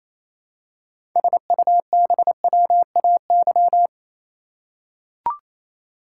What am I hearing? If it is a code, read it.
Morse code, heard as a keyed tone, about 27 wpm, 700 Hz. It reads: SUBWAY
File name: Easy Words - Speed-Racing with Hints - 22wpm